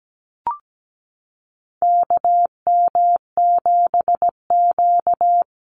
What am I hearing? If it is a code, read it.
Morse code, heard as a keyed tone, about 17 wpm, 700 Hz. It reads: KM7Q
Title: Call Signs - 17wpm